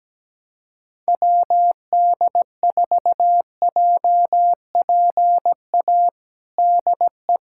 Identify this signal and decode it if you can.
Morse code, heard as a keyed tone, about 17 wpm, 700 Hz. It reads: WD4JPA DE
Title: USA Callsign DE Callsign 17wpm